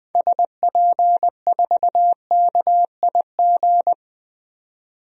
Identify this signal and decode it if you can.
Morse code, heard as a keyed tone, about 21 wpm, 700 Hz. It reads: SP4KIG
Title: DXCC-100 Call Signs - 6 Characters - 20wpm